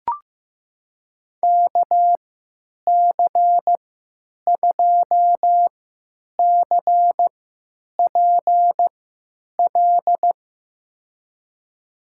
Morse code, heard as a keyed tone, about 15 wpm, spaced as for 9 wpm, 700 Hz. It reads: KC2CPL
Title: USA Call Signs - 6 Characters - 3x Character Spacing - 15wpm